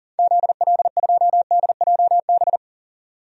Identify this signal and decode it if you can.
Morse code, heard as a keyed tone, about 37 wpm, 700 Hz. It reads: ZL2DJB
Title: DXCC-100 Call Signs - 6 Characters - 40wpm